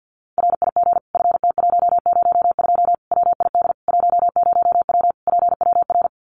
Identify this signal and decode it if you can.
Morse code, heard as a keyed tone, about 53 wpm, 700 Hz. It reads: RIG FT102 WID 10W PWR
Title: Common QSO Phrases 50wpm